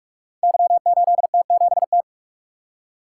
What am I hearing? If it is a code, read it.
Morse code, heard as a keyed tone, about 44 wpm, 690 Hz. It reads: Y8T7T
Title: Code Groups - 5 Letter:Numbers - 45wpm